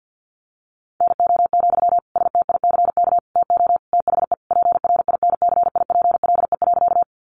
Morse code, heard as a keyed tone, about 50 wpm, 700 Hz. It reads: NO, STICK TO THE PRINCIPLE.
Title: Sentences from Top 1400 Words 50wpm